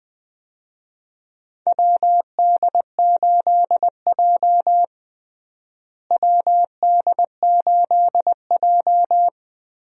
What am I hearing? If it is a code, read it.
Morse code, heard as a keyed tone, about 21 wpm, 700 Hz. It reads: WD8J WD8J
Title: USA Call Signs - 4 Characters - Repeated 3x - 20wpm